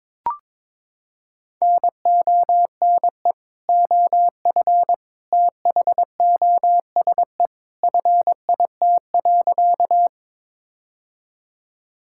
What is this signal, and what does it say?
Morse code, heard as a keyed tone, about 22 wpm, 700 Hz. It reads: NONE OF THOSE FIT.
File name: practice / 22wpm / Sentences with 2-Word Phrases - 22wpm